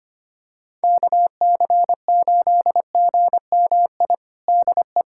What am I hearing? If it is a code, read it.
Morse code, heard as a keyed tone, about 25 wpm, 700 Hz. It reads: KC8GMI DE
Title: USA Callsign DE Callsign - 6 Characters - 25wpm